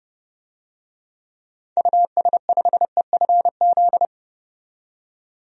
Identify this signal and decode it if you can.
Morse code, heard as a keyed tone, about 30 wpm, 700 Hz. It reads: US5EFZ
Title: DXCC-100 Call Signs - 6 Characters - 30wpm